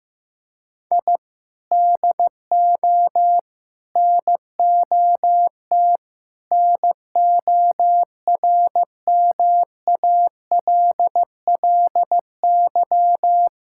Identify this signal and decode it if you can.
Morse code, heard as a keyed tone, about 16 wpm, 700 Hz. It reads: I DO NOT NORMALLY
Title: Sentences from Top 1000 Words - Review Part 6 - 15wpm